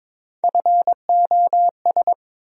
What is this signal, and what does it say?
Morse code, heard as a keyed tone, about 22 wpm, 700 Hz. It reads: FOS
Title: USA Call Sign Postfixes - 3 Letters 22wpm